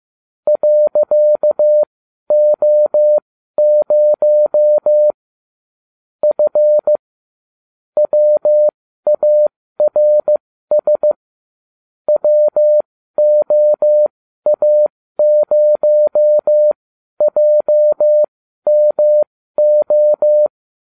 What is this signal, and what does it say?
Morse code, heard as a keyed tone, about 15 wpm, spaced as for 12 wpm, 600 Hz. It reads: .O0 F WARS WOA0JMO